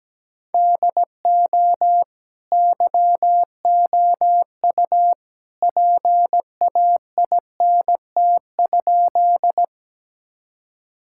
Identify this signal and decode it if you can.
Morse code, heard as a keyed tone, about 17 wpm, 700 Hz. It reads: DO YOU PAINT?
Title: Sentences from Top 1000 Words - Review Part 6 - 17wpm